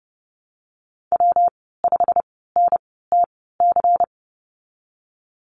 Morse code, heard as a keyed tone, about 30 wpm, spaced as for 17 wpm, 700 Hz. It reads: W5NTC
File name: Lesson 19g - Letter w - Call Signs 30wpm